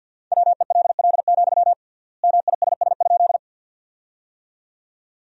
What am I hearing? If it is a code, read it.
Morse code, heard as a keyed tone, about 53 wpm, 700 Hz. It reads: WELL, MISS?
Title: 3x Word-Spacing - Sentences from Top 500 Words - 50wpm